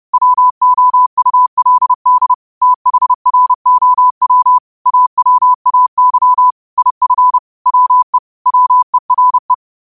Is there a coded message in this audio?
Yes: Morse code from 0.1 to 9.5 s, heard as a keyed tone, about 30 wpm, 1000 Hz. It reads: WOULD THROW AWAY IF WE WERE